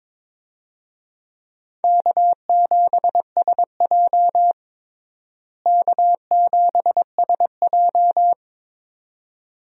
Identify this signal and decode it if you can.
Morse code, heard as a keyed tone, about 22 wpm, 700 Hz. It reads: K7SJ K7SJ
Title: USA Call Signs - 4 Characters - Repeated 2x - 22wpm